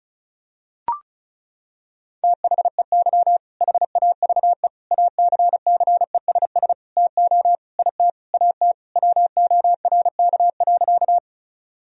Morse code, heard as a keyed tone, about 34 wpm, 700 Hz. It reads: THEY HAVE ACCESS TO IT AT WORK.
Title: Sentences from Top 1000 Words - Review Part 4 - 35wpm